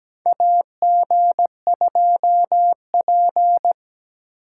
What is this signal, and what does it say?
Morse code, heard as a keyed tone, about 17 wpm, 700 Hz. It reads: AG2P
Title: USA Call Signs - 8x Character Spacing - 17wpm